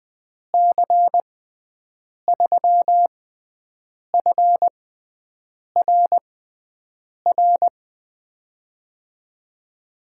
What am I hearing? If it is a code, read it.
Morse code, heard as a keyed tone, about 20 wpm, spaced as for 7 wpm, 700 Hz. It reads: C3FRR
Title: USA Call Signs - 6 Characters - 6x Character Spacing - 20wpm